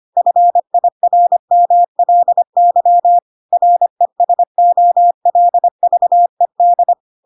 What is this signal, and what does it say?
Morse code, heard as a keyed tone, about 25 wpm, 700 Hz. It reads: FIRMLY RESOLVED
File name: Pinocchio / Pinocchio ch3 25 25 ik1hge